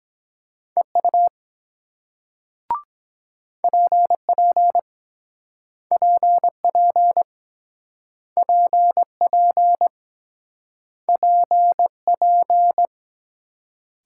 Morse code, heard as a keyed tone, about 24 wpm, 700 Hz. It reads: EU PP PP PP PP